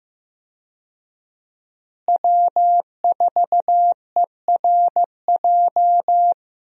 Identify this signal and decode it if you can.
Morse code, heard as a keyed tone, about 15 wpm, 700 Hz. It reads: W4ERJ
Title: USA Call Signs - 5 Characters - Repeated 3x - 15wpm